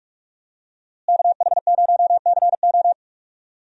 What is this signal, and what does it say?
Morse code, heard as a keyed tone, about 47 wpm, 700 Hz. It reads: KH0CO
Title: USA Call Signs - 5 Characters - Repeated 2x - 45wpm